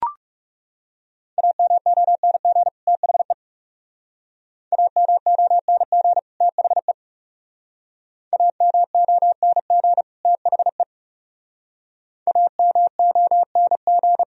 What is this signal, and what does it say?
Morse code, heard as a keyed tone, about 36 wpm, 700 Hz. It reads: AMONG THE AMONG THE AMONG THE AMONG
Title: Speed-Racing - Sets of 2 Words 30wpm